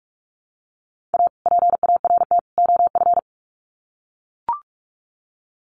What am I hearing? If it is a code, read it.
Morse code, heard as a keyed tone, about 44 wpm, 700 Hz. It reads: A PART OF
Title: Sets of 3 Words 45wpm